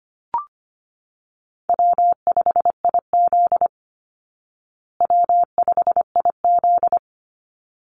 Morse code, heard as a keyed tone, about 25 wpm, 700 Hz. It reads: W5IZ W5IZ
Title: Call Signs Repeated 3x - 25wpm